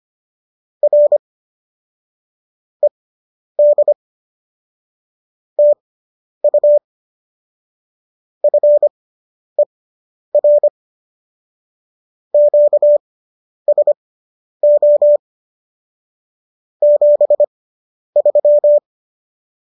Morse code, heard as a keyed tone, about 25 wpm, spaced as for 10 wpm, 590 Hz. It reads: R ED TU FER QSO 73